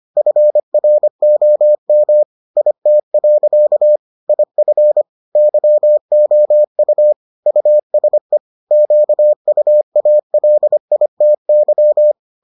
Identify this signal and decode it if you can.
Morse code, heard as a keyed tone, about 25 wpm, 600 Hz. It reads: FROM IT. IF YOU USE QUALITY